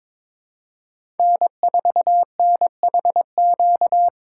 Morse code, heard as a keyed tone, about 22 wpm, 700 Hz. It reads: N4NHQ
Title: USA Call Signs - 2x Character Spacing - 22wpm